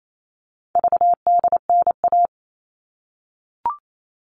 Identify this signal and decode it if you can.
Morse code, heard as a keyed tone, about 28 wpm, 700 Hz. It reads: VDNA
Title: All Cut Numbers - 4-Digit 28wpm